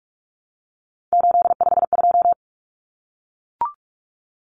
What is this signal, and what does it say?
Morse code, heard as a keyed tone, about 44 wpm, 700 Hz. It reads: 852
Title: Common Cut Numbers - 3-Digit 45wpm